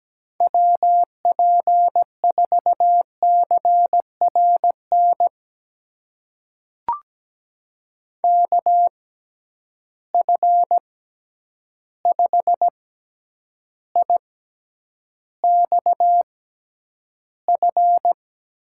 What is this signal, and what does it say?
Morse code, heard as a keyed tone, about 17 wpm, spaced as for 9 wpm, 700 Hz. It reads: WP4CRN KF5IXF
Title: USA Call Signs - 6 Characters - 6x Character Spacing - 17wpm